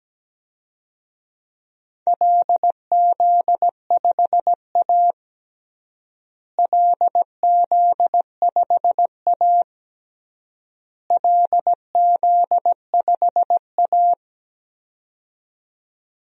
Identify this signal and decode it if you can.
Morse code, heard as a keyed tone, about 17 wpm, 700 Hz. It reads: LZ5A LZ5A LZ5A
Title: DXCC-100 Call Signs - 4 Characters - Repeated 3x - 17wpm